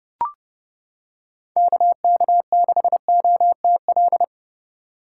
Morse code, heard as a keyed tone, about 31 wpm, 700 Hz. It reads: KK6OTL